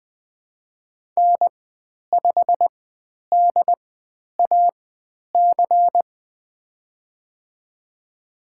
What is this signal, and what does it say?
Morse code, heard as a keyed tone, about 20 wpm, spaced as for 10 wpm, 700 Hz. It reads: N5DAC
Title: 095 Lesson 16g - Letter c - Call Signs 20wpm